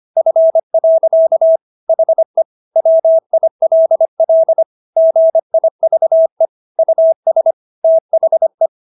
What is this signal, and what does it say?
Morse code, heard as a keyed tone, about 25 wpm, 650 Hz. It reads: F. HE WILL GIVE US THE